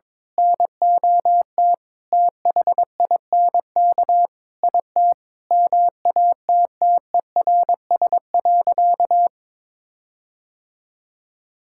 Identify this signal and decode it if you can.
Morse code, heard as a keyed tone, about 22 wpm, 700 Hz. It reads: NOT THINK IT MATTERS.
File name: practice / 22wpm / Sentences with 4-Word Phrases - 22wpm